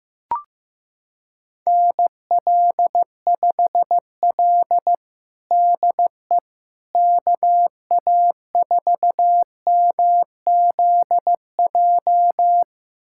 Morse code, heard as a keyed tone, about 15 wpm, 700 Hz. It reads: NL5L DE KA4MZJ
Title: USA Callsign DE Callsign 15wpm